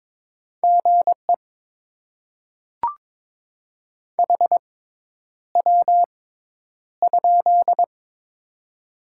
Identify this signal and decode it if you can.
Morse code, heard as a keyed tone, about 22 wpm, spaced as for 10 wpm, 700 Hz. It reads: GE HW?
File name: Top 20 QSO Elements - 6x Character Spacing - 22wpm